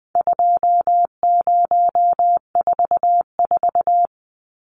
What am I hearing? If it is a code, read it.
Morse code, heard as a keyed tone, about 20 wpm, 700 Hz. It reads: 2044
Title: Numbers - 4-Digit 20wpm